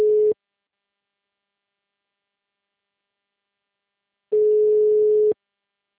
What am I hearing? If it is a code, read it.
Ringback tone (Europe, 425 Hz).
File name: out- -2010- - - .20401